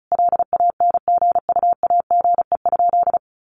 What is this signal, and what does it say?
Morse code, heard as a keyed tone, about 34 wpm, 700 Hz. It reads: LANGUAGE?
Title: Sentences from Top 1100 Words 35wpm